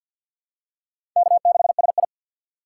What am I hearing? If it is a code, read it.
Morse code, heard as a keyed tone, about 46 wpm, 700 Hz. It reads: K6SI
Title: USA Call Signs - 4 Characters - 6x Character Spacing - 50wpm